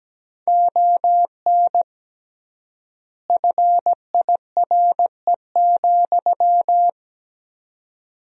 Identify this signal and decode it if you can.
Morse code, heard as a keyed tone, about 17 wpm, 700 Hz. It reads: ON FIRE,
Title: 3x Word-Spacing - Sentences from Top 400 Words - 17wpm